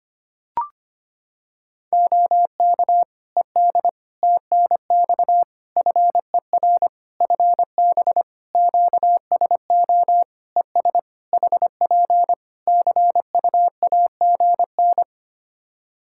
Morse code, heard as a keyed tone, about 25 wpm, 700 Hz. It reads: OK ED TNX FER FB QSO ES HP CUAGN
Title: Practice QSO - Exchange 4 - Ending - 25wpm